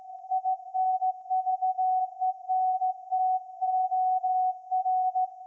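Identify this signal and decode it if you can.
Morse code, heard as a keyed tone, about 15 wpm, 750 Hz. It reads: INVENTOR
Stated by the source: Morse code from SAQ, received at a station in Putignano, Apulia, Italy, on VLF.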